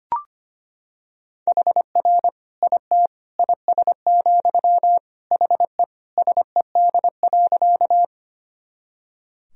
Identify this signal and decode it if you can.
Morse code, heard as a keyed tone, about 25 wpm, 700 Hz. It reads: HR IT IS, HE SED.
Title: Sentences from Top 200 Words and Abbreviations 25wpm